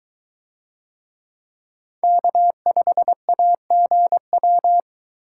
Morse code, heard as a keyed tone, about 22 wpm, 700 Hz. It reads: K5AGW